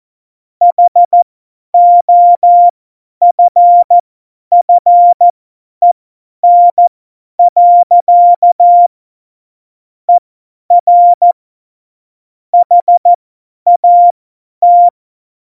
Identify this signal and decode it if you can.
Morse code, heard as a keyed tone, about 14 wpm, spaced as for 10 wpm, 700 Hz. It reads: HOFFEN. ER HAT
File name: Morsekurs / Witze / Witz 12. Dezember 2017, 50 F70